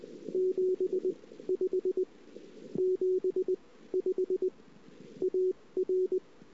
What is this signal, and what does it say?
Morse code, heard as a keyed tone, about 20 wpm, spaced as for 15 wpm, 370 Hz. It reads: 75 75 AR
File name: CW / 2008-ss-cw-k5tr-op-n5rz -02:00